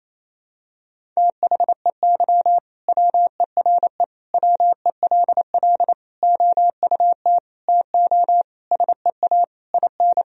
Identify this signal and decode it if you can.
Morse code, heard as a keyed tone, about 28 wpm, 700 Hz. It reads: THEY WERE WELL OUT TO SEA IN
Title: Sentences from Top 1000 Words - Review Part 3 - 28wpm